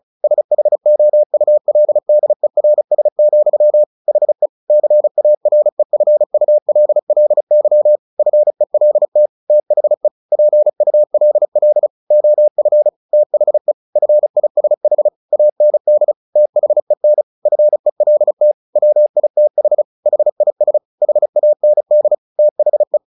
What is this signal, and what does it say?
Morse code, heard as a keyed tone, about 34 wpm, 600 Hz. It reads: SHOULDERS, HE CAREFULLY FELT THE PULL OF THE FISH AND THEN FELT WITH HIS HAND THE
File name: the old man and the sea / the old man and the sea 1 35wpm